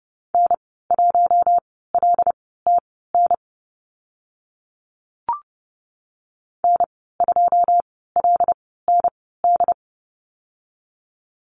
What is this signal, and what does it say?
Morse code, heard as a keyed tone, about 30 wpm, spaced as for 17 wpm, 700 Hz. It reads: N1LTN N2LND
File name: Lesson 14g - Number 2 - Call Signs 30wpm